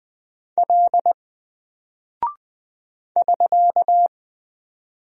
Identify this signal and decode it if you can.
Morse code, heard as a keyed tone, about 20 wpm, 700 Hz. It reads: L <SK>
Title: Single Character (Letter:Number:Punctuation:Prosign) 20wpm